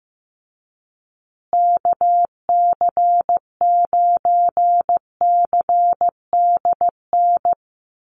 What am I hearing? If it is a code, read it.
Morse code, heard as a keyed tone, about 15 wpm, 700 Hz. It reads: KC9CDN
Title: USA Call Signs Repeated 2x 15wpm